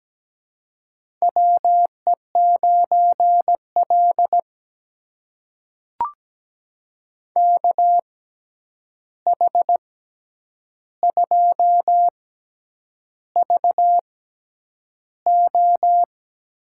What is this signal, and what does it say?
Morse code, heard as a keyed tone, about 17 wpm, spaced as for 8 wpm, 700 Hz. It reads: WE9L KH2VO
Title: USA Call Signs - 6x Character Spacing - 17wpm